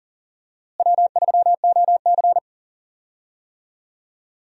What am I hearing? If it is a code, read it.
Morse code, heard as a keyed tone, about 38 wpm, 700 Hz. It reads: W3OC
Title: USA Call Signs - 4 Characters - Repeated 2x - 40wpm